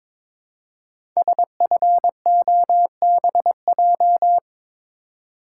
Morse code, heard as a keyed tone, about 22 wpm, 700 Hz. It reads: SFOBJ
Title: Code Groups - 5 Letters - 22wpm